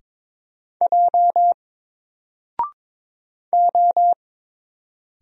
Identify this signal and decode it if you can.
Morse code, heard as a keyed tone, about 22 wpm, 700 Hz. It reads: J O
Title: Single Letter-Number Alt 22wpm